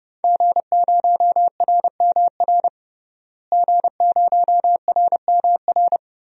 Morse code, heard as a keyed tone, about 31 wpm, 700 Hz. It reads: G0RMR G0RMR
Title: DXCC-100 Call Signs - 5 Characters - Repeated 3x - 30wpm